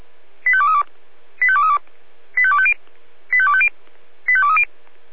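Five-tone selective call (ZVEI), heard as quick runs of five tones, tones of about 70 ms. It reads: 85232 85232 85250 85250 85230
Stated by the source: ZVEI five-tone selective call received with an SDR.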